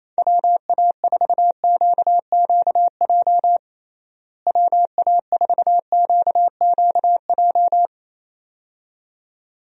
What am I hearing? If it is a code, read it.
Morse code, heard as a keyed tone, about 28 wpm, 700 Hz. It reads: WA4QQJ WA4QQJ
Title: USA Call Signs - 6 Characters - Repeated 2x - 28wpm